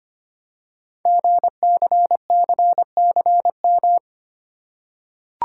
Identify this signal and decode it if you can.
Morse code, heard as a keyed tone, about 25 wpm, 700 Hz. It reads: GCCCM
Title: Code Groups - 5 Letters - 25wpm